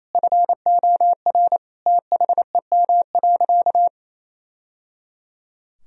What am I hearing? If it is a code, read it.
Morse code, heard as a keyed tone, about 28 wpm, 700 Hz. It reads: FOR THEM.
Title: Sentences from Top 1000 Words - Review Part 3 - 28wpm